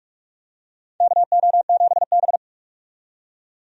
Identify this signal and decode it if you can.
Morse code, heard as a keyed tone, about 46 wpm, 700 Hz. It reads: KO7B
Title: USA Call Signs - 4 Characters - 45wpm